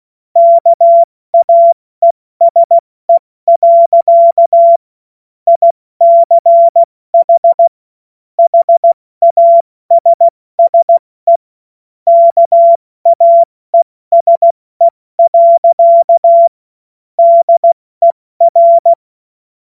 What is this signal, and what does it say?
Morse code, heard as a keyed tone, about 16 wpm, 680 Hz. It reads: KAESE. ICH HASSE KAESE. DER
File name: Morsekurs / Witze / Witz 01. November 2022 70 F80